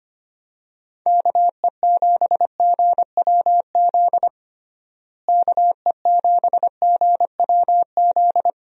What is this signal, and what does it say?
Morse code, heard as a keyed tone, about 25 wpm, 700 Hz. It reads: KE7GWZ KE7GWZ
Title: USA Call Signs - 6 Characters - Repeated 3x - 25wpm